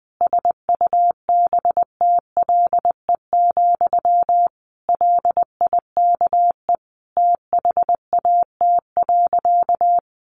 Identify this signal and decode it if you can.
Morse code, heard as a keyed tone, about 20 wpm, 700 Hz. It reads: SUBTLE, LIKE THAT.
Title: A Sound Of Thunder 20wpm